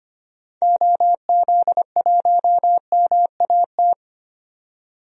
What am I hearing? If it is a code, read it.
Morse code, heard as a keyed tone, about 25 wpm, 700 Hz. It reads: OZ1MAT
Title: DXCC-100 Call Signs - 6 Characters - 25wpm